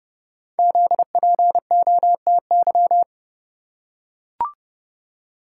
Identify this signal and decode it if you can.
Morse code, heard as a keyed tone, about 30 wpm, 700 Hz. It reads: ZPOTY
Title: Code Groups - 5 Letters - 30wpm